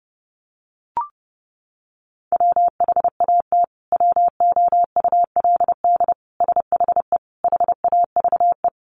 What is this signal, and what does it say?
Morse code, heard as a keyed tone, about 30 wpm, 700 Hz. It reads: WHAT WOULD SHE HAVE
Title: Sentences from Top 100 Words 30wpm